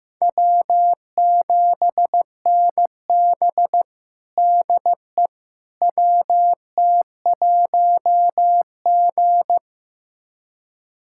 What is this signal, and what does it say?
Morse code, heard as a keyed tone, about 15 wpm, 700 Hz. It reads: W7NB DE WT1G
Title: USA Callsign DE Callsign - 4 Characters - 15wpm